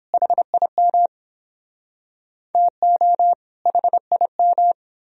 Morse code, heard as a keyed tone, about 29 wpm, 700 Hz. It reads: HIM TO HIM